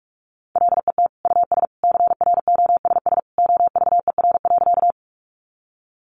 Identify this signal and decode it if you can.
Morse code, heard as a keyed tone, about 44 wpm, 700 Hz. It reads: LET US CROSS OVER.
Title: Sentences from Top 300 Words 45wpm